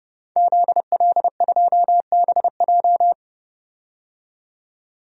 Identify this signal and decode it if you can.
Morse code, heard as a keyed tone, about 30 wpm, 700 Hz. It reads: ZL2BJ
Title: DXCC-100 Call Signs - 5 Characters - 30wpm